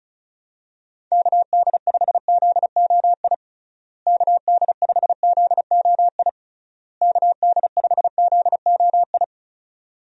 Morse code, heard as a keyed tone, about 36 wpm, 700 Hz. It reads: KD5ZOI KD5ZOI KD5ZOI
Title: USA Call Signs - 6 Characters - Repeated 3x - 35wpm